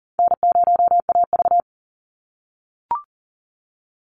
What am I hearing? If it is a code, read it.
Morse code, heard as a keyed tone, about 40 wpm, 700 Hz. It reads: N0AV
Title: Easy Call Signs - 40wpm